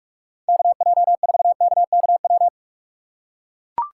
Morse code, heard as a keyed tone, about 44 wpm, 700 Hz. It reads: KJ4KKW
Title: USA Call Signs - 6 Characters - 3x Character Spacing - 45wpm